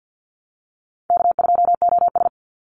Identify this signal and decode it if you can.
Morse code, heard as a keyed tone, about 50 wpm, 700 Hz. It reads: K2OS
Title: Call Signs Repeated 2x - 50wpm